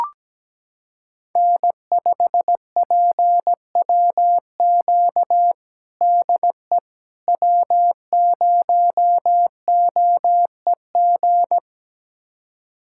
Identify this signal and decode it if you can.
Morse code, heard as a keyed tone, about 17 wpm, 700 Hz. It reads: N5PWQ DE W0OEG
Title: USA Callsign DE Callsign - 5 Characters - 17wpm